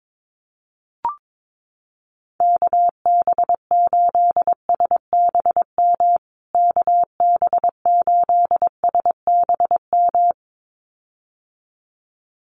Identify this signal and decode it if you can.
Morse code, heard as a keyed tone, about 22 wpm, 700 Hz. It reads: KB8SBM KB8SBM
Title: USA Call Signs Repeated 2x 22wpm